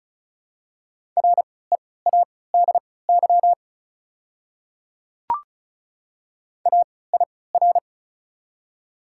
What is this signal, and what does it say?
Morse code, heard as a keyed tone, about 36 wpm, spaced as for 20 wpm, 700 Hz. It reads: READY AIR
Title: 3x Character Spacing - Top 500 Words - 35wpm